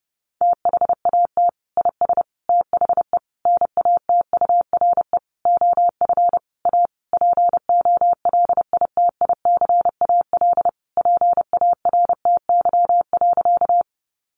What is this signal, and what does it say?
Morse code, heard as a keyed tone, about 30 wpm, 700 Hz. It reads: THAT IS THE NATURE OF A POLITICAL PARTY.
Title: Sentences from Top 1400 Words 30wpm